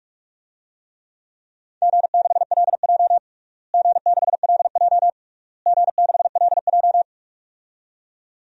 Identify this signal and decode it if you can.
Morse code, heard as a keyed tone, about 42 wpm, 700 Hz. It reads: G6LJ G6LJ G6LJ
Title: DXCC-100 Call Signs - 4 Characters - Repeated 3x - 45wpm